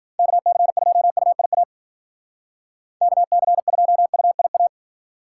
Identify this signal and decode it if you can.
Morse code, heard as a keyed tone, about 48 wpm, 700 Hz. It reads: KC2UIA KC2UIA
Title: Speed Racing - USA 2x3 Call Signs 40wpm